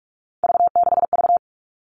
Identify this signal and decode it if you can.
Morse code, heard as a keyed tone, about 44 wpm, 700 Hz. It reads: V6V
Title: All Cut Numbers - 3-Digit 45wpm